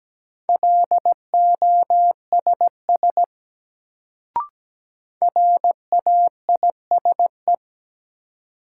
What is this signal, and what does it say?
Morse code, heard as a keyed tone, about 17 wpm, 700 Hz. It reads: LOSS RAISE
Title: Sentences from Top 1400 Words - New Word Drill Alt - 17wpm